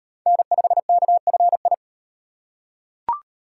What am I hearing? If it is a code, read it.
Morse code, heard as a keyed tone, about 38 wpm, 700 Hz. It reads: N5KFI